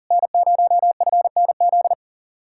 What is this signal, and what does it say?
Morse code, heard as a keyed tone, about 47 wpm, 700 Hz. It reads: N0FNZ
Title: Speed-Racing Common QSO Phrases - 40wpm